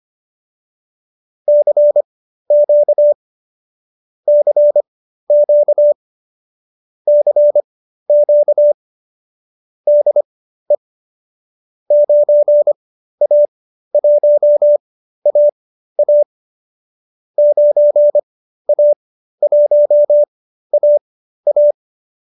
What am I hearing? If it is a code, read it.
Morse code, heard as a keyed tone, about 25 wpm, spaced as for 13 wpm, 590 Hz. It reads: CQ CQ CQ DE 9A1AA 9A1AA